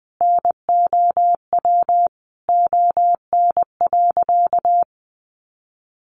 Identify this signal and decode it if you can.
Morse code, heard as a keyed tone, about 20 wpm, 700 Hz. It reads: NOW ON.
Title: Easy Phrases 20wpm